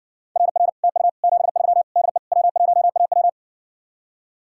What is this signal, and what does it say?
Morse code, heard as a keyed tone, about 60 wpm, 700 Hz. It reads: RR TU 73 DE W1AW